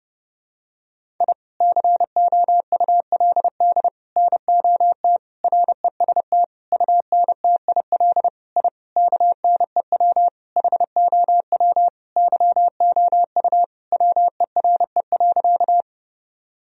Morse code, heard as a keyed tone, about 30 wpm, 700 Hz. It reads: I COULD NOT REST UNTIL I KNEW HOW YOU WERE.
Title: Sentences from Top 1000 Words - Review Part 1 - 30wpm